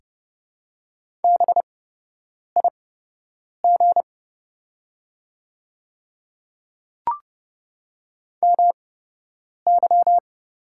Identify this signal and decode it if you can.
Morse code, heard as a keyed tone, about 30 wpm, spaced as for 8 wpm, 700 Hz. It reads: BIG MY
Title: 8x Character Spacing - Top 300 Words - 30wpm